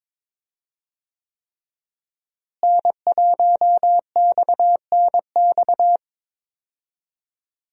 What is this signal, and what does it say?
Morse code, heard as a keyed tone, about 22 wpm, 700 Hz. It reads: N1XNX
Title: USA Call Signs - 5 Characters - Repeated 3x - 22wpm